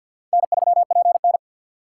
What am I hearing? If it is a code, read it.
Morse code, heard as a keyed tone, about 53 wpm, 700 Hz. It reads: N3PN
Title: USA Call Signs - 4 Characters - 6x Character Spacing - 50wpm